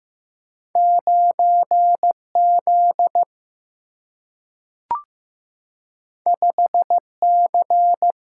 Morse code, heard as a keyed tone, about 15 wpm, 700 Hz. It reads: 9 Z 5 C
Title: Single-Letter Single-Number 15wpm